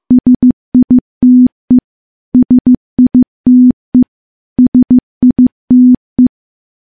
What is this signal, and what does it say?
Morse code, heard as a keyed tone, about 15 wpm, 250 Hz. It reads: SITE SITE SITE
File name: morse-training-15WPM